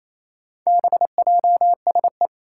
Morse code, heard as a keyed tone, about 27 wpm, 700 Hz. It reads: BJSE
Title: Four Challenging Letter-Number Combinations 28wpm